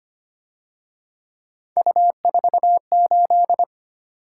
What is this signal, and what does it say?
Morse code, heard as a keyed tone, about 29 wpm, 700 Hz. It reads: U48
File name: Speed-Racing - Three Letter-Number Combinations - 25wpm